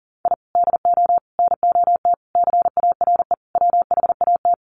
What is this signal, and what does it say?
Morse code, heard as a keyed tone, about 40 wpm, 700 Hz. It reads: I DO NOT CARE WHAT
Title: Sets of 5 Words 40wpm